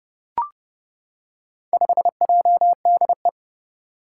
Morse code, heard as a keyed tone, about 30 wpm, 700 Hz. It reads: 5JDE
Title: Four Challenging Letter-Number Combinations 30wpm